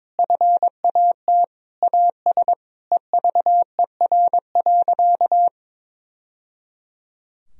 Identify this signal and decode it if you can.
Morse code, heard as a keyed tone, about 22 wpm, 700 Hz. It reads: FAT AS EVER.
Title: Sentences from Top 1000 Words - Review Part 4 - 22wpm